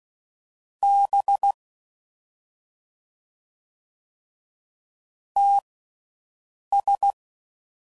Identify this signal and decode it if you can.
Morse code, heard as a keyed tone, about 16 wpm, spaced as for 6 wpm, 800 Hz. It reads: B TS
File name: Mixed / CW Comparisons / Comparisons Broken Characters